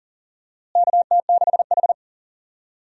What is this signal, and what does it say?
Morse code, heard as a keyed tone, about 40 wpm, 700 Hz. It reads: KT6H
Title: USA Call Signs - 4 Characters - Repeated 2x - 40wpm